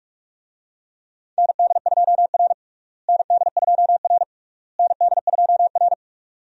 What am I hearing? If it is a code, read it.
Morse code, heard as a keyed tone, about 46 wpm, 700 Hz. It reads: ND2R ND2R ND2R
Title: USA Call Signs - 4 Characters - Repeated 3x - 45wpm